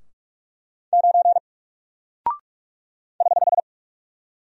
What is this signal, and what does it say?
Morse code, heard as a keyed tone, about 46 wpm, 700 Hz. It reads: 9 <HH>
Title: Single Character (Letter:Number:Punctuation:Prosign) 45wpm